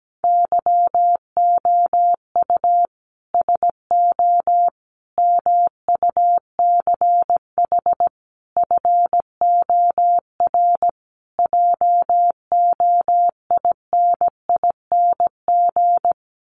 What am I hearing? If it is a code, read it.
Morse code, heard as a keyed tone, about 17 wpm, 700 Hz. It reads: YOU SO MUCH FOR JOINING
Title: Sets of 5 Words Encore2 17wpm